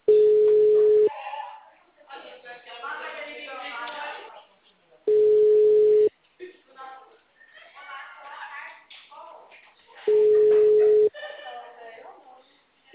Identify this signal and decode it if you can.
Ringback tone (Europe, 425 Hz), with speech.
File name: out- -2012- - - .42094